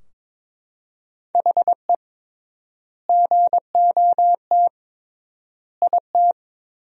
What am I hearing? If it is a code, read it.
Morse code, heard as a keyed tone, about 22 wpm, 700 Hz. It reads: HE GOT IT
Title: 3x Word-Spacing - Sentences from Top 400 Words - 22wpm